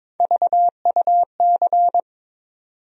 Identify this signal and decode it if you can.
Morse code, heard as a keyed tone, about 22 wpm, 700 Hz. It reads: VUC
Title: USA Call Sign Postfixes - 3 Letters 22wpm